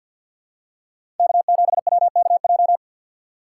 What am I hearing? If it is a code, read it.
Morse code, heard as a keyed tone, about 50 wpm, 700 Hz. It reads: K7WKJ
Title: USA Call Signs - 5 Characters - 4x Character Spacing - 50wpm